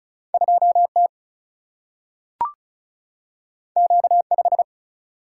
Morse code, heard as a keyed tone, about 36 wpm, 700 Hz. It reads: 2T Q5
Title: Single-Letter Single-Number 35wpm